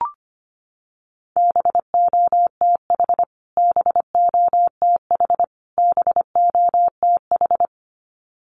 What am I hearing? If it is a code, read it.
Morse code, heard as a keyed tone, about 25 wpm, 700 Hz. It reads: BOTH BOTH BOTH
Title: Top 300 Words Repeated 25wpm